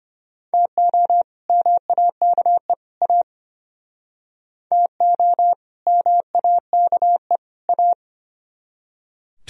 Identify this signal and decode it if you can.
Morse code, heard as a keyed tone, about 34 wpm, 700 Hz. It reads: TO MAKE A TO MAKE A